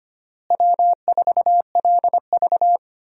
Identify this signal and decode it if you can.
Morse code, heard as a keyed tone, about 25 wpm, 700 Hz. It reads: W4LV
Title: USA Call Signs - 4 Characters - Repeated 2x - 25wpm